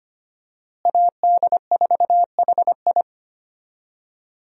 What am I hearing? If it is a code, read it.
Morse code, heard as a keyed tone, about 25 wpm, 700 Hz. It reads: AD4HI
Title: USA Call Signs - 5 Characters - Repeated 3x - 25wpm